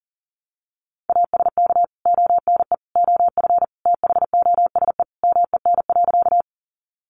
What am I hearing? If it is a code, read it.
Morse code, heard as a keyed tone, about 40 wpm, 700 Hz. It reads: ASK ONE OF THOSE MEN.
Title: Sentences from Top 300 Words 40wpm